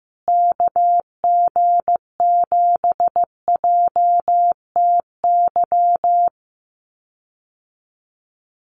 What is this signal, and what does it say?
Morse code, heard as a keyed tone, about 15 wpm, 700 Hz. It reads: KG7JTY
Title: USA Call Signs Repeated 2x 15wpm